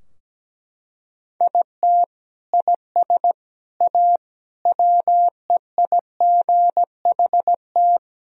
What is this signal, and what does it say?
Morse code, heard as a keyed tone, about 17 wpm, 700 Hz. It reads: IT IS A WEIGHT
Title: Sentences from Top 1000 Words - Review Part 3 - 17wpm